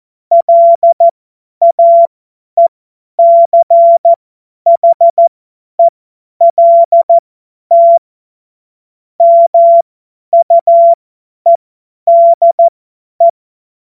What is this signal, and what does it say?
Morse code, heard as a keyed tone, about 14 wpm, spaced as for 10 wpm, 680 Hz. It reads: LAECHELT MUEDE
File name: Morsekurs / Witze / Witz 14. Juni 2022 50 F70